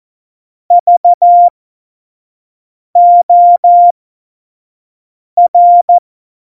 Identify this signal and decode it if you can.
Morse code, heard as a keyed tone, about 14 wpm, spaced as for 5 wpm, 700 Hz. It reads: VOR
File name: Morsekurs / Witze / Witz 24 April 2018, 25 F70